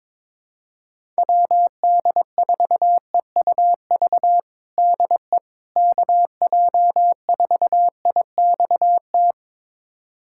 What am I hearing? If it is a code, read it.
Morse code, heard as a keyed tone, about 22 wpm, 700 Hz. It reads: WD4EUV DE KJ4IXT
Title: USA Callsign DE Callsign - 6 Characters - 22wpm